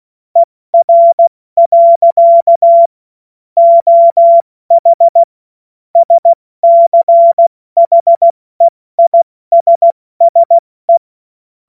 Morse code, heard as a keyed tone, about 16 wpm, 680 Hz. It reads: ER. OH SCHEISSE
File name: Morsekurs / Witze / Witz 07. März 2023 70 F80